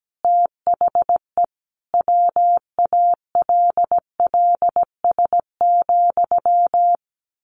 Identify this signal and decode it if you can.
Morse code, heard as a keyed tone, about 17 wpm, 700 Hz. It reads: THE WALLS,
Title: A Sound Of Thunder 17wpm